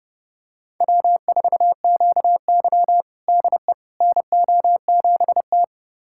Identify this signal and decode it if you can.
Morse code, heard as a keyed tone, about 30 wpm, 700 Hz. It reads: W4QY DE NO7T
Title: USA Callsign DE Callsign - 4 Characters - 30wpm